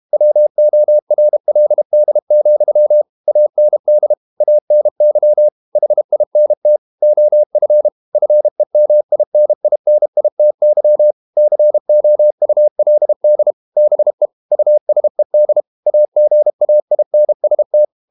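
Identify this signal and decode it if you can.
Morse code, heard as a keyed tone, about 32 wpm, 600 Hz. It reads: WORLD, AND ANY HINT OF FEMININITY COULD BE USED AGAINST
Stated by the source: Morse code practice at 32 wpm from CW med Gnister, reading Ken Follett's novel Never in English.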